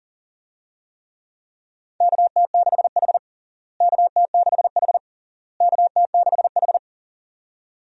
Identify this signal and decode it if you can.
Morse code, heard as a keyed tone, about 37 wpm, 700 Hz. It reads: KT6H KT6H KT6H
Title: USA Call Signs - 4 Characters - Repeated 3x - 40wpm